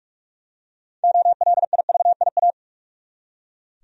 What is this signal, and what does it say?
Morse code, heard as a keyed tone, about 42 wpm, 700 Hz. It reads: OLIVIA
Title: USA Common Names - 45wpm